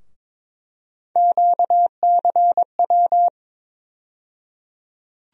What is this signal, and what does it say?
Morse code, heard as a keyed tone, about 22 wpm, 700 Hz. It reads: QCW
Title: USA Call Sign Postfixes - 3 Letters 22wpm